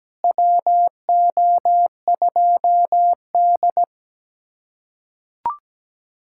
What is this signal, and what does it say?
Morse code, heard as a keyed tone, about 17 wpm, 700 Hz. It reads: WO2D
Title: USA Callsign DE Callsign - 4 Characters - 17wpm